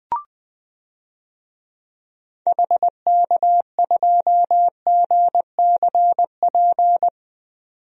Morse code, heard as a keyed tone, about 20 wpm, 700 Hz. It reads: HK2GCP
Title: DXCC-100 Call Signs - 6 Characters - 20wpm